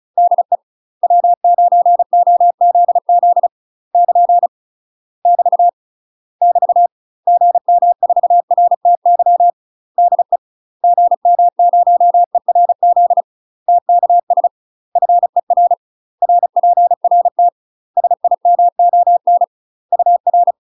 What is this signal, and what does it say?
Morse code, heard as a keyed tone, about 34 wpm, 700 Hz. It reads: DE W9OZZ <KN> <BT> <BT> GM4RTY DE GM0ERZ TKS FER RPRT SIMON UR
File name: UK MASTER QSOs TRAINING / 35WPM / UK MASTER QSOs 0003